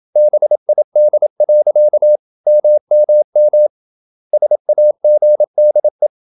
Morse code, heard as a keyed tone, about 27 wpm, 600 Hz. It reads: BID. MMM SAGDE
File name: Ferie QRQ 27wpm